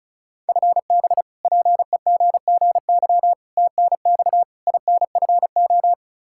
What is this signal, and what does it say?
Morse code, heard as a keyed tone, about 36 wpm, 700 Hz. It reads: FB PEGGY TNX INFO
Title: Practice QSO - Exchange 3 - More Info - 35wpm